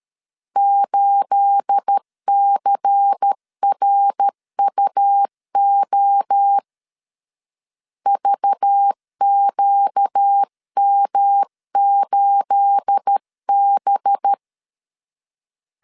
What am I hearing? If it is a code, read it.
Morse code, heard as a keyed tone, about 13 wpm, 800 Hz. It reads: 8CRUO VQM8B
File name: LEZIONI / 9 28 12wpm 2351